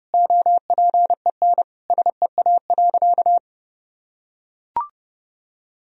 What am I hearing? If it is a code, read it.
Morse code, heard as a keyed tone, about 30 wpm, 700 Hz. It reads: OPEN SEA.
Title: Sentences from Top 1000 Words - Review Part 1 - 30wpm